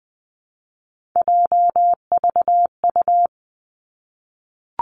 Morse code, heard as a keyed tone, about 20 wpm, 700 Hz. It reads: JVU
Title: Three Challenging Letter-Number Combinations 20wpm